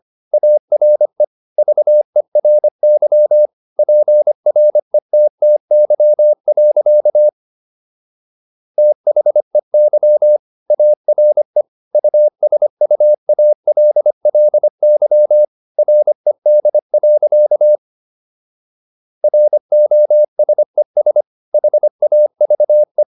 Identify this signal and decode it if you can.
Morse code, heard as a keyed tone, about 25 wpm, 600 Hz. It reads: ARE VERY PRETTY. THEY ARE USUALLY RED. ROSES HAVE